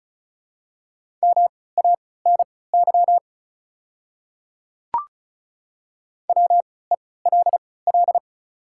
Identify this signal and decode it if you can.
Morse code, heard as a keyed tone, about 36 wpm, spaced as for 20 wpm, 700 Hz. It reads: MANY WELL
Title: 3x Character Spacing - Top 300 Words - 35wpm